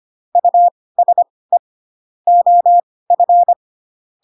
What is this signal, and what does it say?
Morse code, heard as a keyed tone, about 25 wpm, spaced as for 18 wpm, 700 Hz. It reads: USE OF
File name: Pinocchio / Pinocchio ch1 12 25 ik1hge